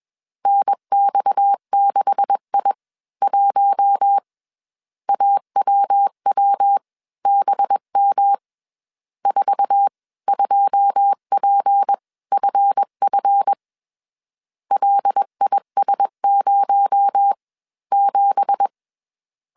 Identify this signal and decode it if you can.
Morse code, heard as a keyed tone, about 21 wpm, spaced as for 17 wpm, 800 Hz. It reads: NX6I1 AWWBM 42PFF LIS07